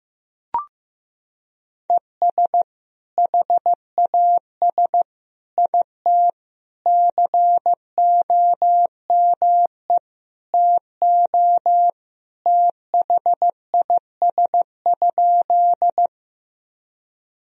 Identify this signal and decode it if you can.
Morse code, heard as a keyed tone, about 15 wpm, 700 Hz. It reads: ES HAS IT COME TO THIS?
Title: Sentences from Top 100 Words and Abbreviations 15wpm